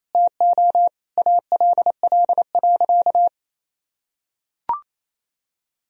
Morse code, heard as a keyed tone, about 28 wpm, 700 Hz. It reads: TO ALL.
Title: Sentences from Top 1000 Words - Review Part 1 - 28wpm